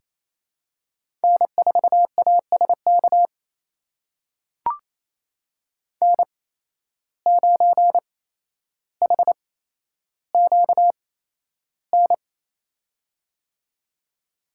Morse code, heard as a keyed tone, about 28 wpm, spaced as for 12 wpm, 700 Hz. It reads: N4ASK N9HQN
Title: USA Call Signs - 5 Characters - 8x Character Spacing - 28wpm